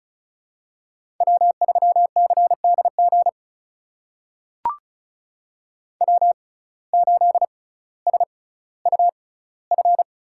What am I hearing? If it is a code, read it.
Morse code, heard as a keyed tone, about 36 wpm, spaced as for 18 wpm, 700 Hz. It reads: W3CDG W8SUF
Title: USA Call Signs - 5 Characters - 6x Character Spacing - 35wpm